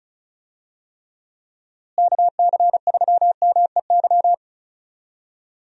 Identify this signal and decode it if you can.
Morse code, heard as a keyed tone, about 34 wpm, 700 Hz. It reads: KC3MEY
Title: USA Call Signs - 6 Characters - Repeated 3x - 35wpm